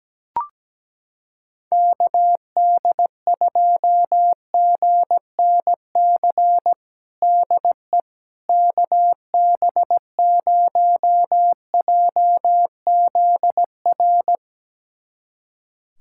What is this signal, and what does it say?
Morse code, heard as a keyed tone, about 17 wpm, 700 Hz. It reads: KD2GNC DE KB0JZR
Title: USA Callsign DE Callsign - 6 Characters - 17wpm